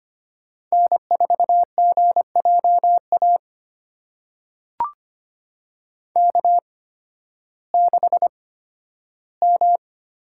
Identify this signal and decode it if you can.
Morse code, heard as a keyed tone, about 25 wpm, spaced as for 13 wpm, 700 Hz. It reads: N4GJA K6M
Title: USA Call Signs - 5 Characters - 8x Character Spacing - 25wpm